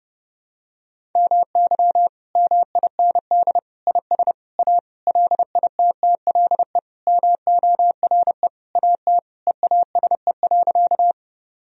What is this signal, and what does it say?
Morse code, heard as a keyed tone, about 29 wpm, 700 Hz. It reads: MY MIND IS A LITTLE MORE AT EASE.
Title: Sentences from Top 1000 Words - Review Part 1 - 30wpm